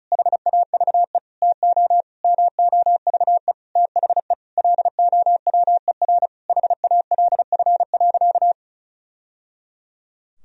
Morse code, heard as a keyed tone, about 36 wpm, 700 Hz. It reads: HAVE TO MOVE THE LOWER HALF.
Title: Sentences from Top 1000 Words - Review Part 4 - 35wpm